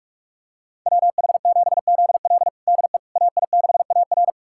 Morse code, heard as a keyed tone, about 41 wpm, 700 Hz. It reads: WH7ZL DE AI6AR
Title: USA Callsign DE Callsign - 5 Characters - 45wpm